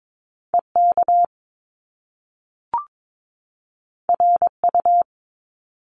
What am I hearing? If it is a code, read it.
Morse code, heard as a keyed tone, about 22 wpm, 700 Hz. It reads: EK RU
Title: Top Two Letter Combinations in Words 22wpm